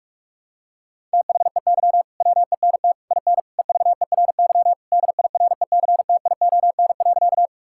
Morse code, heard as a keyed tone, about 42 wpm, 700 Hz. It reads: THEY WENT IN EVERY DIRECTION.
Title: Sentences from Top 1000 Words - Review Part 5 - 45wpm